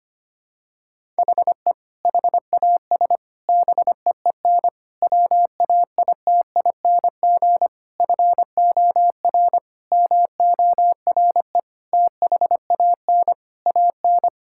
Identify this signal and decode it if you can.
Morse code, heard as a keyed tone, about 25 wpm, 700 Hz. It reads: HE HAS BEEN WAITING FOR MORE THAN AN
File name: practice / 25wpm / Sentences with 3-Word Phrases - 25wpm